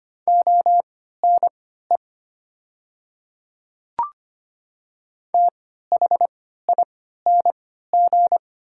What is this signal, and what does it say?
Morse code, heard as a keyed tone, about 25 wpm, spaced as for 14 wpm, 700 Hz. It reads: ONE THING
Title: 3x Character Spacing - Top 100 Words - 25wpm